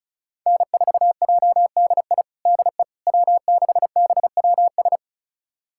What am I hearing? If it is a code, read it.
Morse code, heard as a keyed tone, about 36 wpm, 700 Hz. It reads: N4JDI DE W6BWS
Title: USA Callsign DE Callsign - 5 Characters - 35wpm